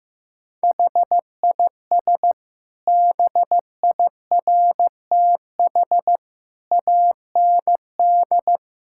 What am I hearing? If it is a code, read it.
Morse code, heard as a keyed tone, about 16 wpm, 700 Hz. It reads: HIS BIRTH AND
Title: Sentences from Top 1000 Words - Review Part 5 - 15wpm